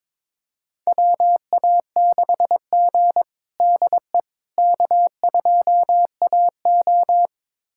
Morse code, heard as a keyed tone, about 21 wpm, 700 Hz. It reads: WA6G DE K2AO
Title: USA Callsign DE Callsign - 4 Characters - 22wpm